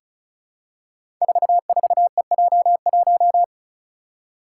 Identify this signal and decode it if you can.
Morse code, heard as a keyed tone, about 36 wpm, 700 Hz. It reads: 44EJ1
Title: Code Groups - 5 Letter:Numbers - 35wpm